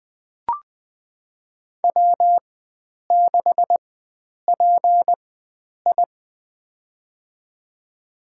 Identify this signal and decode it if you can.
Morse code, heard as a keyed tone, about 21 wpm, spaced as for 9 wpm, 700 Hz. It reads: W6PI
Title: USA Call Signs - 4 Characters - 4x Character Spacing - 20wpm